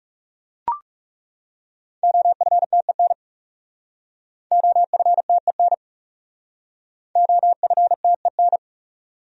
Morse code, heard as a keyed tone, about 37 wpm, 700 Hz. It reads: OFTEN OFTEN OFTEN
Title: Speed-Racing Top 500 Words 30wpm